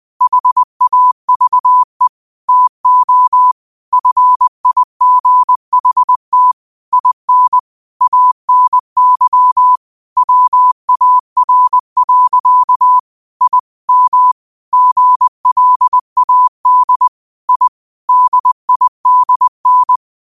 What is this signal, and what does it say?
Morse code, heard as a keyed tone, about 20 wpm, 1000 Hz. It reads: HAVE TO FIGHT IN ANY WAR. I M GLAD I DIDN
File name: qotdfiles / 20wpm